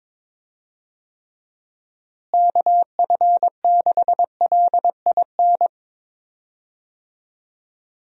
Morse code, heard as a keyed tone, about 22 wpm, 700 Hz. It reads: KF6LIN
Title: USA Call Signs - 6 Characters - Repeated 2x - 22wpm